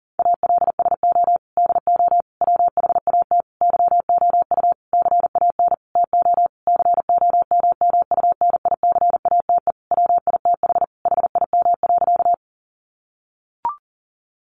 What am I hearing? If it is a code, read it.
Morse code, heard as a keyed tone, about 40 wpm, 700 Hz. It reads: ALSO DO WHAT YOU CAN TO COMMUNICATE WITH HIM.
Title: Sentences from Top 1300 Words 40wpm